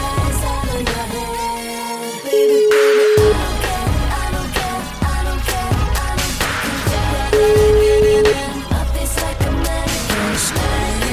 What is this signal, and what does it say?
Ringback tone (Europe, 425 Hz), with noise.